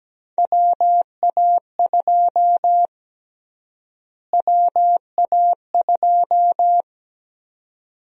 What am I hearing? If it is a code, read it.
Morse code, heard as a keyed tone, about 17 wpm, 700 Hz. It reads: WA2 WA2
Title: USA Call Sign Prefixes - Repeated 2x - 17wpm